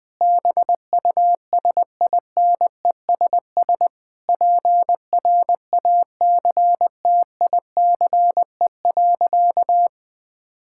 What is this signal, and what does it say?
Morse code, heard as a keyed tone, about 21 wpm, 700 Hz. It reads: BUSINESS PRACTICE.
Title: Sentences from Top 1000 Words - Review Part 6 - 20wpm